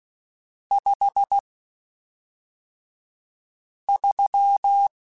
Morse code, heard as a keyed tone, about 16 wpm, spaced as for 3 wpm, 800 Hz. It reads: 53
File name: Spaced Characters / Spaced Characters-mp3s / Weather5 1 sp.@5